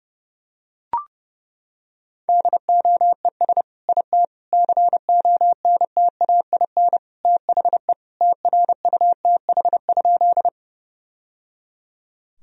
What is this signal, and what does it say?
Morse code, heard as a keyed tone, about 30 wpm, 700 Hz. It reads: DOES IT CONTAIN THE TRUTH?
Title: Sentences from Top 1000 Words - Review Part 2 - 30wpm